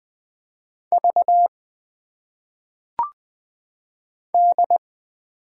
Morse code, heard as a keyed tone, about 20 wpm, 700 Hz. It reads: V D
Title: Single Letters - Phonetic Pronunciation 20wpm